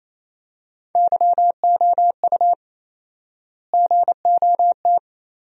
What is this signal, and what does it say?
Morse code, heard as a keyed tone, about 27 wpm, 700 Hz. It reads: YOU GOT
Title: 4x Word-Spacing - Sentences from Top 300 Words - 28wpm